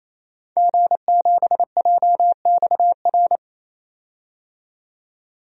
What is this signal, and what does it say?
Morse code, heard as a keyed tone, about 28 wpm, 700 Hz. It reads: G7JXR
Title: DXCC-100 Call Signs - 5 Characters - 28wpm